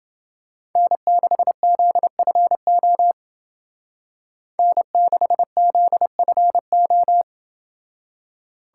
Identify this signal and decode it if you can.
Morse code, heard as a keyed tone, about 28 wpm, 700 Hz. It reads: N6ZFO N6ZFO